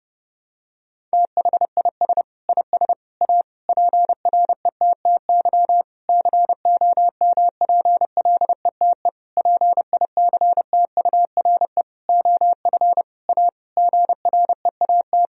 Morse code, heard as a keyed tone, about 30 wpm, 700 Hz. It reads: THIS IS A PRETTY COMPLETE PICTURE OF A GREAT
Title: Sentences from Top 1000 Words - Review Part 2 - 30wpm